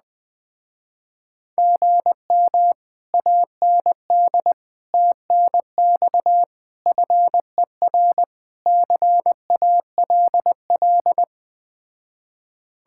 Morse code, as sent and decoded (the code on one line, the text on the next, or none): --. -- / .- -. -.. / - -. -..- / ..-. .-. / -.-. .- .-.. .-..
GM AND TNX FER CALL